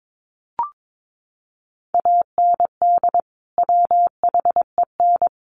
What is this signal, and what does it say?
Morse code, heard as a keyed tone, about 22 wpm, 700 Hz. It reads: AND WHEN
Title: Sentences from Top 100 Words 22wpm